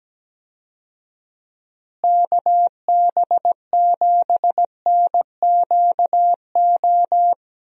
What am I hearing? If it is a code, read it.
Morse code, heard as a keyed tone, about 17 wpm, 700 Hz. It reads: KB7NQO
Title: USA Call Signs - 6 Characters - Repeated 3x - 17wpm